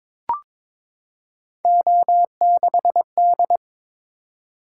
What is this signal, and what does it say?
Morse code, heard as a keyed tone, about 22 wpm, 700 Hz. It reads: O6D
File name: practice / 22wpm / Three Letter-Number Combinations - 22wpm